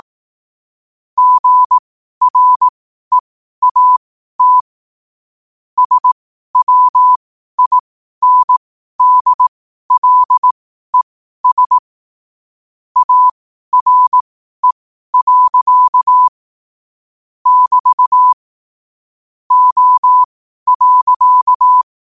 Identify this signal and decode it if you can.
Morse code, heard as a keyed tone, about 18 wpm, spaced as for 12 wpm, 1000 Hz. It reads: GREAT SWINDLES ARE. <BT> O.